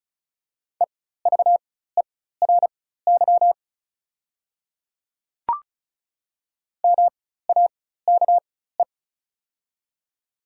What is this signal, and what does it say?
Morse code, heard as a keyed tone, about 36 wpm, spaced as for 16 wpm, 700 Hz. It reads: EVERY MAKE
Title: 4x Character Spacing - Top 300 Words - 35wpm